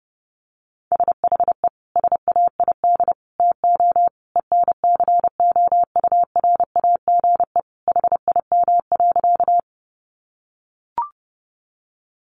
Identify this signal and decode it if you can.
Morse code, heard as a keyed tone, about 30 wpm, 700 Hz. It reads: SHE SAID TO ENCOURAGE HIM.
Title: Sentences from Top 1400 Words 30wpm